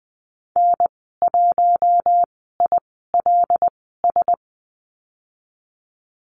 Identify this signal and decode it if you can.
Morse code, heard as a keyed tone, about 20 wpm, spaced as for 15 wpm, 700 Hz. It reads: N1ILS
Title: 082 Lesson 14g - Number 2 - Call Signs 20wpm